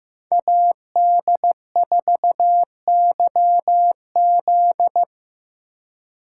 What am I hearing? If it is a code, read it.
Morse code, heard as a keyed tone, about 15 wpm, 700 Hz. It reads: AD4YZ
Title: USA Call Signs - 5 Characters - 2x Character Spacing - 15wpm